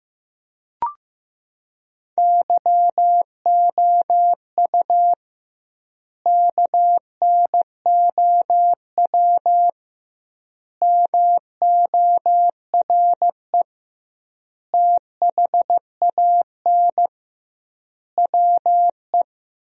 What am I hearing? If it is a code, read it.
Morse code, heard as a keyed tone, about 16 wpm, 700 Hz. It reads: YOU KNOW MORE THAN WE
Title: 2x Word-Spacing - Sentences from Top 100 Words - 15wpm